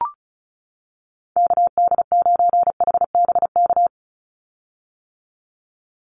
Morse code, heard as a keyed tone, about 34 wpm, 700 Hz. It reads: KD9HBK
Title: USA Call Signs 35wpm